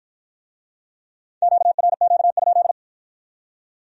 Morse code, heard as a keyed tone, about 47 wpm, 700 Hz. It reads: QRQ?